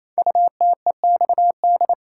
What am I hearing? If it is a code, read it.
Morse code, heard as a keyed tone, about 27 wpm, 700 Hz. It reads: UTEXD
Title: Code Groups - 5 Letters - 28wpm